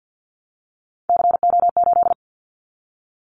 Morse code, heard as a keyed tone, about 50 wpm, 700 Hz. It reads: CO8
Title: Callsign Prefixes for Top 100 DXCC - 50wpm